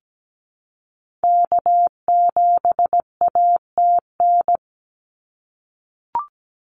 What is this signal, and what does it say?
Morse code, heard as a keyed tone, about 17 wpm, 700 Hz. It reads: K7ATN
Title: NA SOTA Activator Callsigns 17wpm